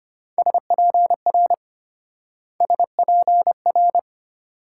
Morse code, heard as a keyed tone, about 32 wpm, 700 Hz. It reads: SPR SPR